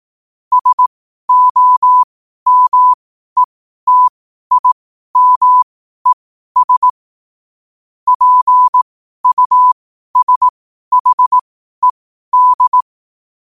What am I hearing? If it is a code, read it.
Morse code, heard as a keyed tone, about 18 wpm, spaced as for 12 wpm, 1000 Hz. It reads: SOMETIMES PUSHED
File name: qotdfiles / 12wpm